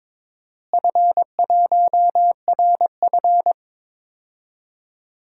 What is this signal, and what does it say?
Morse code, heard as a keyed tone, about 22 wpm, 700 Hz. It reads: F1RF
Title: DXCC-100 Call Signs - 4 Characters - Repeated 3x - 22wpm